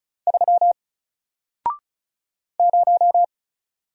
Morse code, heard as a keyed tone, about 36 wpm, 700 Hz. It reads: 3 0
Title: Single Letter-Number Alt 35wpm